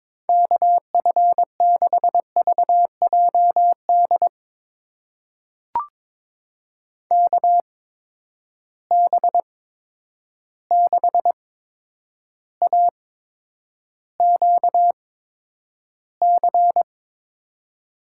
Morse code, heard as a keyed tone, about 21 wpm, spaced as for 9 wpm, 700 Hz. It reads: KF6VJD KB6AQC
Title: USA Call Signs - 6 Characters - 8x Character Spacing - 22wpm